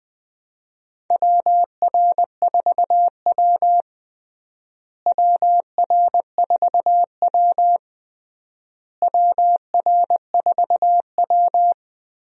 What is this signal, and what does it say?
Morse code, heard as a keyed tone, about 21 wpm, 700 Hz. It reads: WR4W WR4W WR4W
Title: USA Call Signs - 4 Characters - Repeated 3x - 20wpm